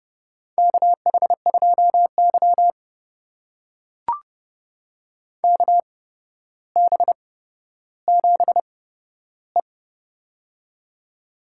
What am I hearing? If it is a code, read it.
Morse code, heard as a keyed tone, about 30 wpm, spaced as for 13 wpm, 700 Hz. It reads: KH2Y KB7E
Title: USA Call Signs - 4 Characters - 8x Character Spacing - 30wpm